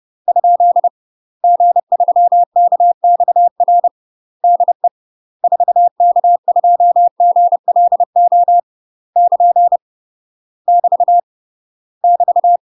Morse code, heard as a keyed tone, about 30 wpm, 700 Hz. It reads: ? G3KXR DE 4K2GLO <KN> <BT> <BT>
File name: UK MASTER QSOs TRAINING / 30WPM / UK MASTER QSOs 0026